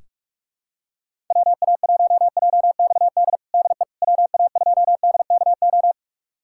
Morse code, heard as a keyed tone, about 41 wpm, 700 Hz. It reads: WA1JXD DE WA2DKO
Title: USA Callsign DE Callsign - 6 Characters - 45wpm